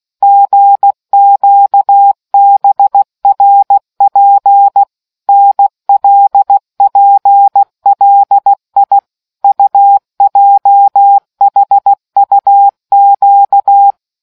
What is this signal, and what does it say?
Morse code, heard as a keyed tone, about 16 wpm, 800 Hz. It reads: GQBRP NLPLI UJHUQ